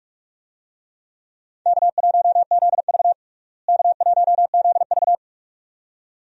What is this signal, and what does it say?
Morse code, heard as a keyed tone, about 47 wpm, 700 Hz. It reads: K1ZV K1ZV
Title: USA Call Signs - 4 Characters - Repeated 2x - 45wpm